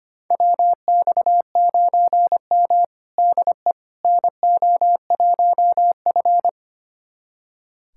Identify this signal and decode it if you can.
Morse code, heard as a keyed tone, about 25 wpm, 700 Hz. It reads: WX9M DE NO1F
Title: USA Callsign DE Callsign - 4 Characters - 25wpm